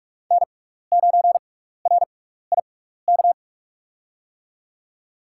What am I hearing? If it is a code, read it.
Morse code, heard as a keyed tone, about 46 wpm, spaced as for 15 wpm, 700 Hz. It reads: N9RIK
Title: USA Call Signs - 5 Characters - 6x Character Spacing - 45wpm